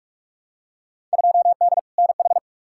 Morse code, heard as a keyed tone, about 46 wpm, 700 Hz. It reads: 2D NH
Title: Field Day Exchanges Encore - 45wpm